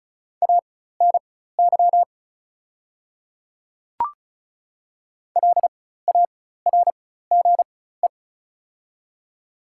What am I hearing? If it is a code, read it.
Morse code, heard as a keyed tone, about 34 wpm, spaced as for 16 wpm, 690 Hz. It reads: ANY LARGE
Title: 4x Character Spacing - Top 200 Words - 35wpm